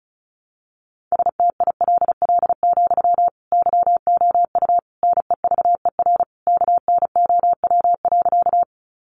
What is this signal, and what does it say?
Morse code, heard as a keyed tone, about 34 wpm, 700 Hz. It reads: STILL, YOU NEVER KNOW.
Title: Sentences from Top 300 Words 35wpm